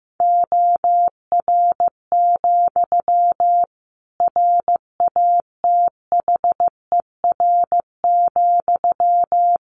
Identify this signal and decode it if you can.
Morse code, heard as a keyed tone, about 15 wpm, 700 Hz. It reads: OR, RATHER,
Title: Sentences from Top 700 Words 15wpm